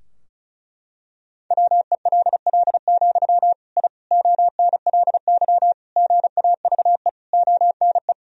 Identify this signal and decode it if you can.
Morse code, heard as a keyed tone, about 36 wpm, 700 Hz. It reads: WELL, I ONLY GAVE ONE
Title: Sentences from Top 1000 Words - Review Part 2 - 35wpm